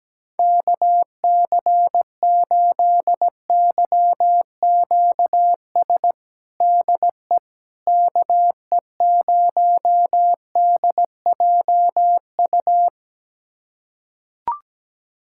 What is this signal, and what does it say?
Morse code, heard as a keyed tone, about 17 wpm, 700 Hz. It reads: KC8YQS DE KE0DJU
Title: USA Callsign DE Callsign 17wpm